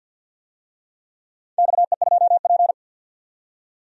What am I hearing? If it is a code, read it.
Morse code, heard as a keyed tone, about 47 wpm, 700 Hz. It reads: XE2P
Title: DXCC-100 Call Signs - 4 Characters - 50wpm